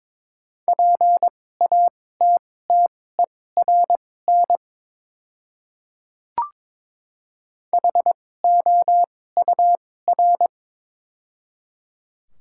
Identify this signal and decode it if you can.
Morse code, heard as a keyed tone, about 22 wpm, spaced as for 16 wpm, 690 Hz. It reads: PATTERN HOUR
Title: 2x Character Spacing - Top 500 Words - 22wpm